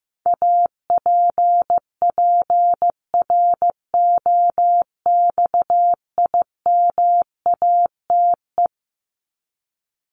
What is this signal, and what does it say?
Morse code, heard as a keyed tone, about 15 wpm, 700 Hz. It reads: APPROXIMATE
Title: LY Suffix - Top 100 Words with Base Words 15wpm